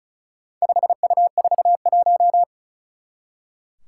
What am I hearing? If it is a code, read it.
Morse code, heard as a keyed tone, about 34 wpm, 700 Hz. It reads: 5U41
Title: Four Challenging Letter-Number Combinations 35wpm